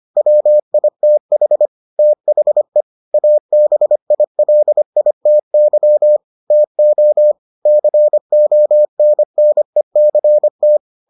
Morse code, heard as a keyed tone, about 25 wpm, 600 Hz. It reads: WITH THE ABILITY TO CONNECT